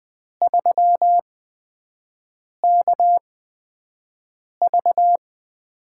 Morse code, heard as a keyed tone, about 20 wpm, spaced as for 5 wpm, 700 Hz. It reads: 3KV
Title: USA Call Signs - 8x Character Spacing - 20wpm